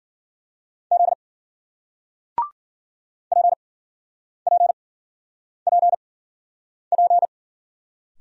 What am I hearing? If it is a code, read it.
Morse code, heard as a keyed tone, about 50 wpm, spaced as for 9 wpm, 700 Hz. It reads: Z PPPP